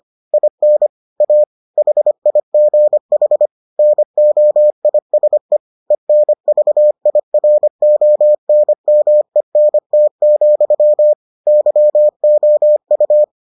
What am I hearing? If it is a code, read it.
Morse code, heard as a keyed tone, about 25 wpm, 600 Hz. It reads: IN A HIGH NOISE ENVIRONMENT, YOU